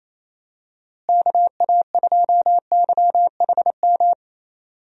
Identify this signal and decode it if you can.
Morse code, heard as a keyed tone, about 28 wpm, 700 Hz. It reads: KA2YHM
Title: USA Call Signs - 6 Characters - Repeated 3x - 28wpm